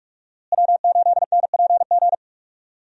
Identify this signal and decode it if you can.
Morse code, heard as a keyed tone, about 46 wpm, 700 Hz. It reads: W8NPG
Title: USA Call Signs - 5 Characters - Repeated 3x - 45wpm